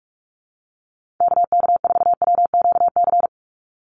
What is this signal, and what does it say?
Morse code, heard as a keyed tone, about 44 wpm, 700 Hz. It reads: KK4WQC
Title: USA Call Signs 45wpm